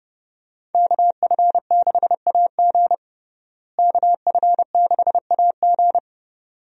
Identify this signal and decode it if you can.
Morse code, heard as a keyed tone, about 30 wpm, 700 Hz. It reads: KF6AG KF6AG
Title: USA Call Signs - 5 Characters - Repeated 2x - 30wpm